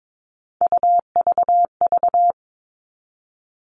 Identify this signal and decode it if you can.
Morse code, heard as a keyed tone, about 22 wpm, 700 Hz. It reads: UVV
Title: All Cut Numbers - 3-Digit 22wpm